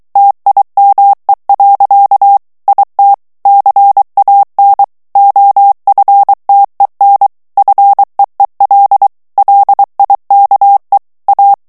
Morse code, heard as a keyed tone, about 23 wpm, 800 Hz. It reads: TIME. IT CAN OFTEN FEEL LIKE A
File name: morse mp3 / ANGLAIS texte clair 1 vitesse 22 mots minute